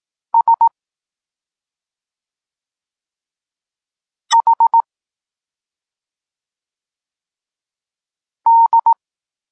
Morse code, heard as a keyed tone, about 18 wpm, 920 Hz. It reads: S H D